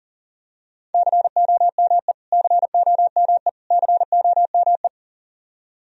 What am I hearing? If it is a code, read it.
Morse code, heard as a keyed tone, about 40 wpm, 700 Hz. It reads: COME COME COME
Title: Top 100 Words Repeated 40wpm